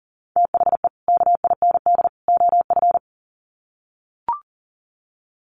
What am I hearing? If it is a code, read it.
Morse code, heard as a keyed tone, about 40 wpm, 700 Hz. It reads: THE KIND OF
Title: Sets of 3 Words 40wpm